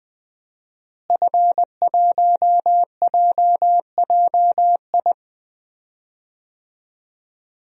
Morse code, heard as a keyed tone, about 20 wpm, 700 Hz. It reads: F1JJI
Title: DXCC-100 Call Signs - 5 Characters - Repeated 3x - 20wpm